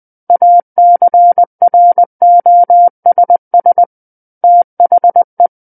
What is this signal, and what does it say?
Morse code, heard as a keyed tone, about 20 wpm, 700 Hz. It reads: ACROSS THE
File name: 20 WPM